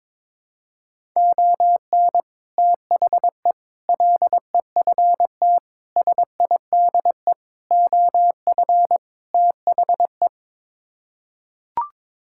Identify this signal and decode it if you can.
Morse code, heard as a keyed tone, about 21 wpm, 700 Hz. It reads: ON THE LEFT SIDE OF THE
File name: practice / 22wpm / Sets of 6 Words - 22wpm